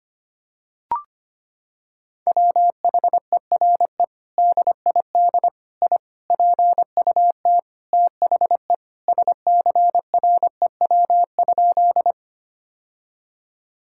Morse code, heard as a keyed tone, about 25 wpm, 700 Hz. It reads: WHERE DID I PUT THE SCREW?
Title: Sentences from Top 1600 Words 25wpm